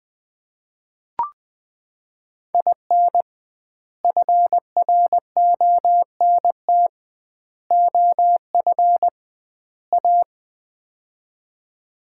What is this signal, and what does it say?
Morse code, heard as a keyed tone, about 20 wpm, 700 Hz. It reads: IN FRONT OF A
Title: 2x Word-Spacing - Sets of 4 Words - 20wpm